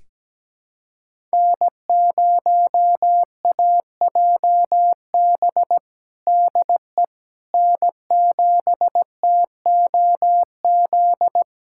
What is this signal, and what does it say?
Morse code, heard as a keyed tone, about 17 wpm, 700 Hz. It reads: N0AJB DE N7TOZ
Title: USA Callsign DE Callsign - 5 Characters - 17wpm